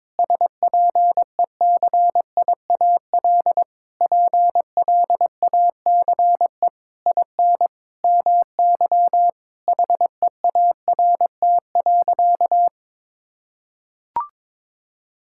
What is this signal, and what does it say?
Morse code, heard as a keyed tone, about 22 wpm, 700 Hz. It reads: SPECIAL PLACE IN MY HEART.
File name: Sentences with 9-Word Phrases - 22wpm